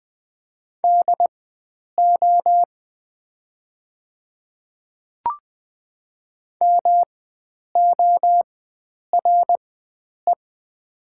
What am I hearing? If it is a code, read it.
Morse code, heard as a keyed tone, about 20 wpm, spaced as for 9 wpm, 700 Hz. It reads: DO MORE
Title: 4x Character Spacing - Top 100 Words - 20wpm